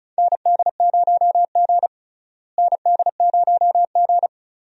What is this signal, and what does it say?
Morse code, heard as a keyed tone, about 36 wpm, 700 Hz. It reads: ND0G ND0G
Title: USA Call Signs - 4 Characters - Repeated 2x - 35wpm